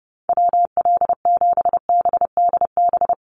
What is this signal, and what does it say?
Morse code, heard as a keyed tone, about 30 wpm, 700 Hz. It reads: WL7BDB
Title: USA Call Signs Repeated 2x 30wpm